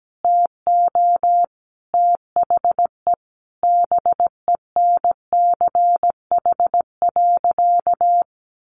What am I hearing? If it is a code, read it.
Morse code, heard as a keyed tone, about 17 wpm, 700 Hz. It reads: TO THE BENCH.
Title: Sentences from Top 1400 Words 17wpm